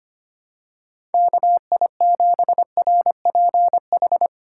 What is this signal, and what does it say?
Morse code, heard as a keyed tone, about 25 wpm, 700 Hz. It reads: KI7RPH
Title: USA Call Signs - 6 Characters - 4x Character Spacing - 25wpm